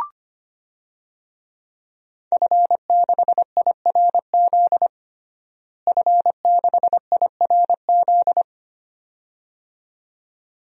Morse code, heard as a keyed tone, about 25 wpm, 700 Hz. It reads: F6IRZ F6IRZ
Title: DXCC-100 Call Signs - 5 Characters - Repeated 2x - 25wpm